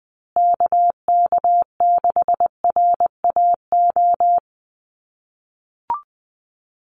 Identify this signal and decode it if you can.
Morse code, heard as a keyed tone, about 20 wpm, 700 Hz. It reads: KK6RAO
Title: USA Call Signs Repeated 3x 20wpm